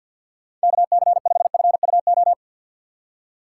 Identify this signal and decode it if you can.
Morse code, heard as a keyed tone, about 52 wpm, 700 Hz. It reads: KC5FUO
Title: USA Call Signs - 6 Characters - 4x Character Spacing - 50wpm